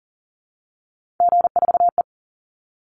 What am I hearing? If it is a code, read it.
Morse code, heard as a keyed tone, about 40 wpm, 700 Hz. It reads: G4E
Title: Call Signs Repeated 3x - 40wpm